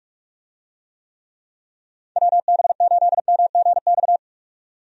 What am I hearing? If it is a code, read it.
Morse code, heard as a keyed tone, about 46 wpm, 700 Hz. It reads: WB8MGX
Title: USA Call Signs - 6 Characters - 45wpm